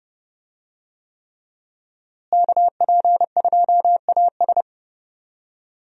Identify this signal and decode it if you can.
Morse code, heard as a keyed tone, about 31 wpm, 700 Hz. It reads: KP2AS
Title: USA Call Signs - 5 Characters - Repeated 3x - 30wpm